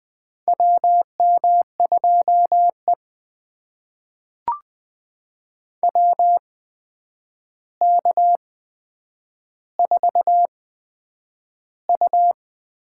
Morse code, heard as a keyed tone, about 20 wpm, spaced as for 9 wpm, 700 Hz. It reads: WM2E WK4U
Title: USA Call Signs - 4 Characters - 8x Character Spacing - 20wpm